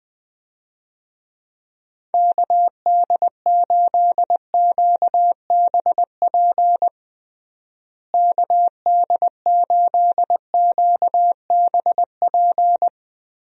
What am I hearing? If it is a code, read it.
Morse code, heard as a keyed tone, about 20 wpm, 700 Hz. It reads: KD8QBP KD8QBP
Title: USA Call Signs - 6 Characters - Repeated 3x - 20wpm